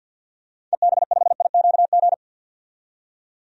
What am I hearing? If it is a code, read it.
Morse code, heard as a keyed tone, about 53 wpm, 700 Hz. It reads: EB5IQG
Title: DXCC-100 Call Signs - 6 Characters - 50wpm